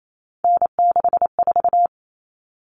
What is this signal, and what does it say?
Morse code, heard as a keyed tone, about 28 wpm, 700 Hz. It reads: N64
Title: Common Cut Numbers - 3-Digit 28wpm